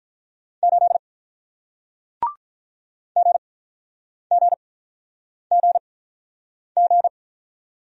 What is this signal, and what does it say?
Morse code, heard as a keyed tone, about 48 wpm, 700 Hz. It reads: 8 G G G G